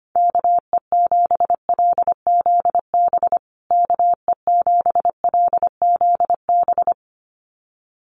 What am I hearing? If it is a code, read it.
Morse code, heard as a keyed tone, about 25 wpm, 700 Hz. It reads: KE7LZB KE7LZB
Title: USA Call Signs Repeated 2x 25wpm